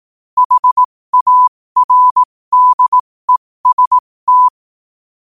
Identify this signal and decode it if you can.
Morse code, heard as a keyed tone, about 18 wpm, spaced as for 15 wpm, 1000 Hz. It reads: HARDEST
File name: qotdfiles / 15wpm